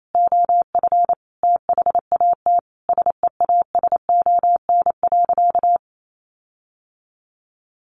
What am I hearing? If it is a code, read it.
Morse code, heard as a keyed tone, about 28 wpm, 700 Hz. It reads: OF THAT SEASON.
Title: Sentences from Top 1400 Words 28wpm